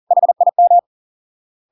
Morse code, heard as a keyed tone, about 40 wpm, 700 Hz. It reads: HIM